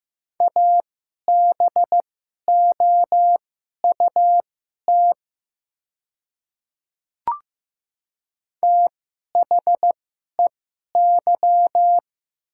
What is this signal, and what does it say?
Morse code, heard as a keyed tone, about 15 wpm, spaced as for 11 wpm, 700 Hz. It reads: ABOUT THEY
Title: 2x Character Spacing - Top 300 Words - 15wpm